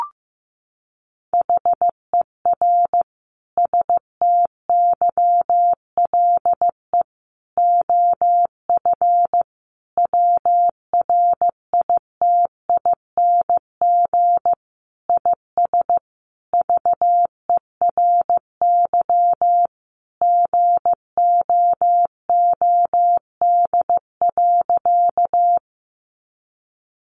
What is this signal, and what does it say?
Morse code, heard as a keyed tone, about 15 wpm, 700 Hz. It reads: HER STYLE OF WRITING IS VERY GOOD.
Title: Sentences from Top 800 Words 15wpm